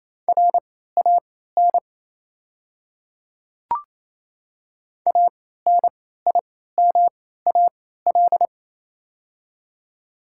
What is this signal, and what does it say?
Morse code, heard as a keyed tone, about 28 wpm, spaced as for 16 wpm, 690 Hz. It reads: RAN ANIMAL
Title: 3x Character Spacing - Top 500 Words - 28wpm